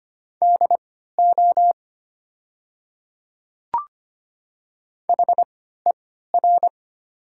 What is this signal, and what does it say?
Morse code, heard as a keyed tone, about 25 wpm, spaced as for 14 wpm, 700 Hz. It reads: DO HER
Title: 3x Character Spacing - Top 100 Words - 25wpm